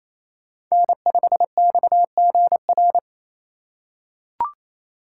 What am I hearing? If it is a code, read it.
Morse code, heard as a keyed tone, about 28 wpm, 700 Hz. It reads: N5XGR
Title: USA Call Signs - 5 Characters - 2x Character Spacing - 28wpm